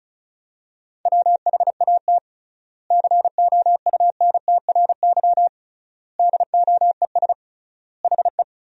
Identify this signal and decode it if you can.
Morse code, heard as a keyed tone, about 36 wpm, 700 Hz. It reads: WHAT COUNTRY DOES HE
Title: 3x Word-Spacing - Sentences from Top 300 Words - 35wpm